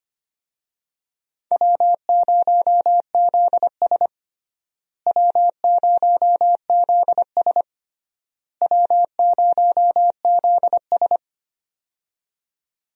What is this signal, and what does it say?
Morse code, heard as a keyed tone, about 25 wpm, 700 Hz. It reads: W0ZS W0ZS W0ZS
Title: USA Call Signs - 4 Characters - Repeated 3x - 25wpm